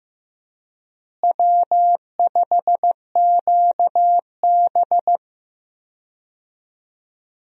Morse code, heard as a keyed tone, about 16 wpm, 700 Hz. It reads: W5QB
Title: USA Call Signs - 4 Characters - Repeated 3x - 15wpm